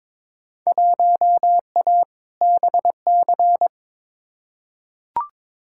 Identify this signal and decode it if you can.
Morse code, heard as a keyed tone, about 22 wpm, 700 Hz. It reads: 1A BC
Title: Field Day Exchanges - 22wpm